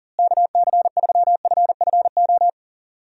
Morse code, heard as a keyed tone, about 40 wpm, 700 Hz. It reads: KC3FFO
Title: USA Call Signs - 6 Characters - Repeated 3x - 40wpm